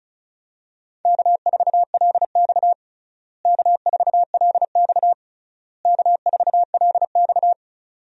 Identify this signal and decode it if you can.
Morse code, heard as a keyed tone, about 34 wpm, 700 Hz. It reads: K4LX K4LX K4LX
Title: USA Call Signs - 4 Characters - Repeated 3x - 35wpm